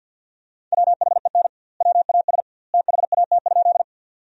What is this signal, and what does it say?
Morse code, heard as a keyed tone, about 50 wpm, 700 Hz. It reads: WHEN WAS THAT?
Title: 2x Word-Spacing - Sentences from Top 100 Words - 50wpm